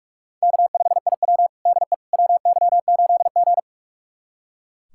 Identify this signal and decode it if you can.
Morse code, heard as a keyed tone, about 41 wpm, 700 Hz. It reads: K5IW DE WY8G
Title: USA Callsign DE Callsign - 4 Characters - 45wpm